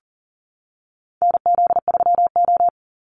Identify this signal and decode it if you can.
Morse code, heard as a keyed tone, about 40 wpm, 700 Hz. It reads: NZ3O
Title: Call Signs - 40wpm